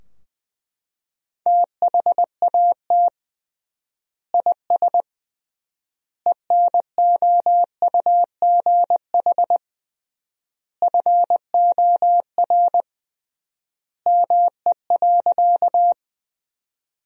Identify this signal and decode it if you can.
Morse code, heard as a keyed tone, about 20 wpm, 700 Hz. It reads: THAT IS ENOUGH FOR ME.
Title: 3x Word-Spacing - Sentences from Top 400 Words - 20wpm